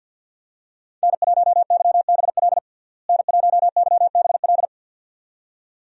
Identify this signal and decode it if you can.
Morse code, heard as a keyed tone, about 52 wpm, 700 Hz. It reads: N1YBL N1YBL
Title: USA Call Signs - 5 Characters - Repeated 2x - 50wpm